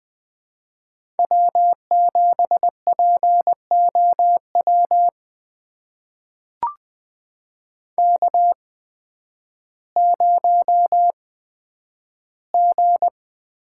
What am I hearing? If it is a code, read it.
Morse code, heard as a keyed tone, about 20 wpm, spaced as for 11 wpm, 700 Hz. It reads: W7POW K0G
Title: USA Call Signs - 5 Characters - 8x Character Spacing - 20wpm